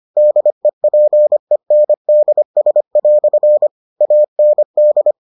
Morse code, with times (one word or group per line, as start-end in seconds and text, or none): 0.2-3.7 DEPENDS"
4.0-5.1 AND